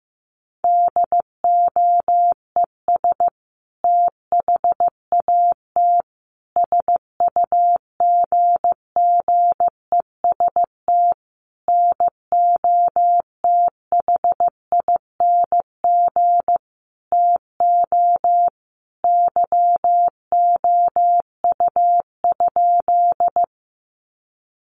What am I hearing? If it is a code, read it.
Morse code, heard as a keyed tone, about 15 wpm, 700 Hz. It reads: DOES THAT SUGGEST NOTHING TO YOU?
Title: Sentences from Top 1300 Words 15wpm